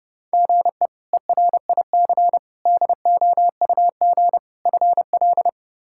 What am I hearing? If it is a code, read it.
Morse code, heard as a keyed tone, about 30 wpm, 700 Hz. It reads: GE ERIC DOUG FL
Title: K1USN SST QSO Practice - 30wpm